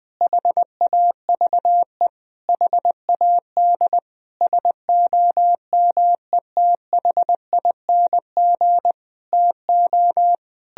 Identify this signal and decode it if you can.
Morse code, heard as a keyed tone, about 21 wpm, 700 Hz. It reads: HAVE HAD SOMETHING TO
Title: Sentences from Top 1000 Words - Review Part 2 - 20wpm